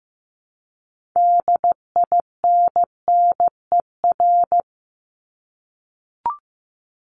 Morse code, heard as a keyed tone, about 15 wpm, 700 Hz. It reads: DINNER
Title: Top 1751-2000 Words 15wpm